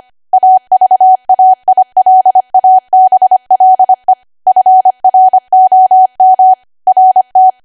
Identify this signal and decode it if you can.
Morse code, heard as a keyed tone, about 25 wpm, 750 Hz. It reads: AVAILABLE FROM RT
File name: Archive / 25 WPM / 25WPM